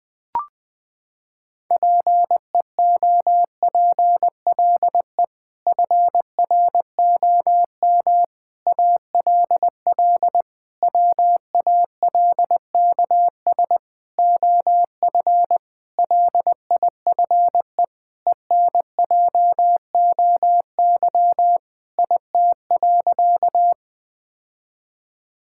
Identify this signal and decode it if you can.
Morse code, heard as a keyed tone, about 20 wpm, 700 Hz. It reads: PEOPLE FROM ALL WALKS OF LIFE ENJOY IT.
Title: Sentences with 6-Word Phrases - 20wpm